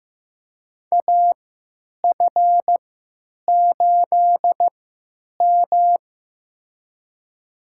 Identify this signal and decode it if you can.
Morse code, heard as a keyed tone, about 15 wpm, spaced as for 9 wpm, 700 Hz. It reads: AF8M
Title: USA Call Signs - 4 Characters - 3x Character Spacing - 15wpm